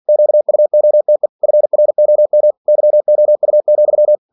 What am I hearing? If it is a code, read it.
Morse code, heard as a keyed tone, about 46 wpm, 600 Hz. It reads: QUOTE FROM YOU,
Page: Never 47wpm